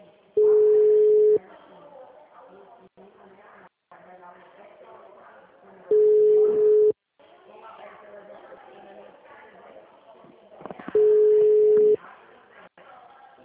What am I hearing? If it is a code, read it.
Ringback tone (Europe, 425 Hz), with speech.
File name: out- -2006- - - .82322